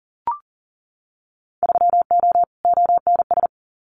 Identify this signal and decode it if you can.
Morse code, heard as a keyed tone, about 38 wpm, 700 Hz. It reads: 3O ONS
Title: Winter Field Day Exchanges 40wpm